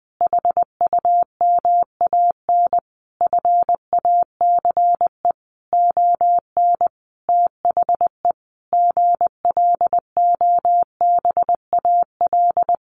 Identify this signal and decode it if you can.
Morse code, heard as a keyed tone, about 19 wpm, 700 Hz. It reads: HUMAN FACE ON THE GLOBAL
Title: Sentences from Top 1200 Words 20wpm